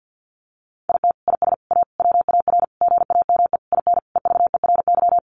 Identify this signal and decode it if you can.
Morse code, heard as a keyed tone, about 50 wpm, 700 Hz. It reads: IT IS A WAR GAME IN EVERY
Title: Sentences from Top 1500 Words 50wpm